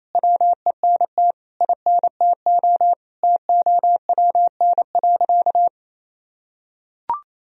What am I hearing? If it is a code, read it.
Morse code, heard as a keyed tone, about 28 wpm, 700 Hz. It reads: WENT INTO TOWN.
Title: Sentences from Top 1000 Words - Review Part 1 - 28wpm